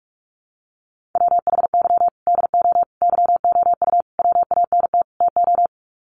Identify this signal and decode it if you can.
Morse code, heard as a keyed tone, about 44 wpm, 700 Hz. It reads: WHY DO YOU WANT TO
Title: Sets of 5 Words Encore2 45wpm